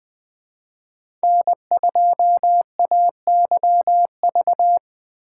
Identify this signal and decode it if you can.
Morse code, heard as a keyed tone, about 20 wpm, 700 Hz. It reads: N2AYV
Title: USA Call Signs - 5 Characters - 2x Character Spacing - 20wpm